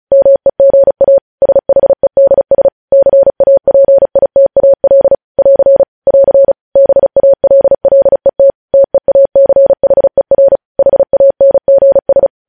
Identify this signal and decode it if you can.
Morse code, heard as a keyed tone, about 36 wpm, 550 Hz. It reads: MEGA SHEDS CAPITAL <AR> <AR> BALLET TEACHER HANGS
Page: News Headlines in Morse Code at 35 WPM